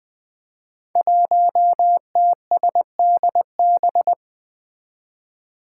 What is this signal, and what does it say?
Morse code, heard as a keyed tone, about 20 wpm, 700 Hz. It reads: 1TSDB
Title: Five Challenging Letter-Number Combinations 20wpm